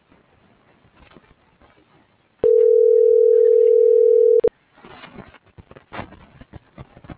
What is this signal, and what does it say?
Ringback tone (North America), 440 + 480 Hz.